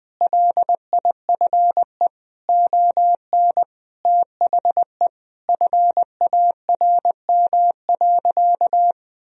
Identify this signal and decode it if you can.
Morse code, heard as a keyed tone, about 20 wpm, 700 Hz. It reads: LIFE ON THE FARM.
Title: Sentences from Top 1000 Words - Review Part 2 - 20wpm